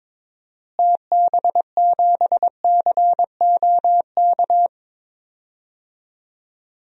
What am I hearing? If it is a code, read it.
Morse code, heard as a keyed tone, about 22 wpm, 700 Hz. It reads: TB7COK
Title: DXCC-100 Call Signs - 6 Characters - Repeated 3x - 22wpm